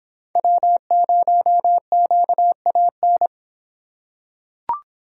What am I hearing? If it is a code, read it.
Morse code, heard as a keyed tone, about 26 wpm, 700 Hz. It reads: W0QAN